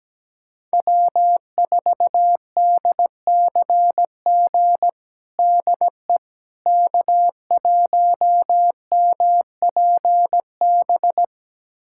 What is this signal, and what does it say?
Morse code, heard as a keyed tone, about 17 wpm, 700 Hz. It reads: W4DCG DE K1MPB
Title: USA Callsign DE Callsign - 5 Characters - 17wpm